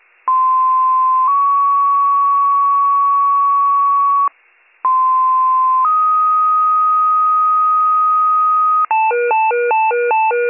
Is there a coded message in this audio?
Yes: two-tone page from 0.3 to 8.9 s, heard as a one-second tone, then a three-second tone, about 1040 then 1120 Hz, about 1040 then 1250 Hz.